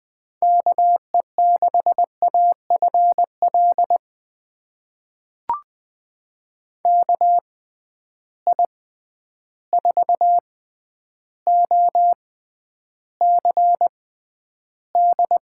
Morse code, heard as a keyed tone, about 20 wpm, spaced as for 10 wpm, 700 Hz. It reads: KE6AFL KI4OCD
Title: USA Call Signs - 6 Characters - 6x Character Spacing - 20wpm